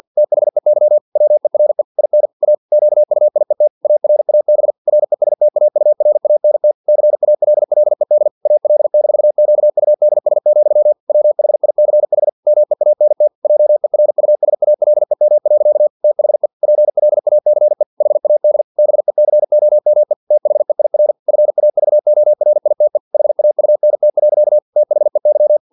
Morse code, heard as a keyed tone, about 50 wpm, 600 Hz. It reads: THEY WERE IN A QUIET ARAB RESTAURANT CALLED AL-QUDS, WHICH MEANT JERUSALEM. THE PLACE HAD BECOME THEIR FAVORITE HAUNT. THEY
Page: Never 49wpm